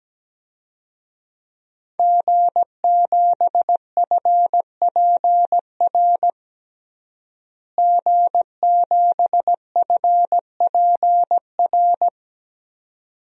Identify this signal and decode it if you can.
Morse code, heard as a keyed tone, about 17 wpm, 700 Hz. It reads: G7FPR G7FPR
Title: DXCC-100 Call Signs - 5 Characters - Repeated 2x - 17wpm